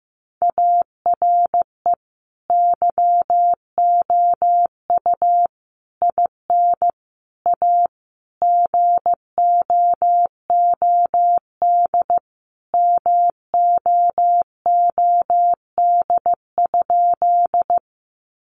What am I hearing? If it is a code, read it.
Morse code, heard as a keyed tone, about 15 wpm, 700 Hz. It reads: ARE YOU IN A GOOD MOOD?
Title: Sentences from Top 1100 Words 15wpm